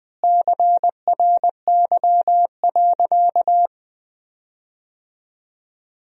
Morse code, heard as a keyed tone, about 20 wpm, 700 Hz. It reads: CRY.
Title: 4x Word-Spacing - Sentences from Top 500 Words - 20wpm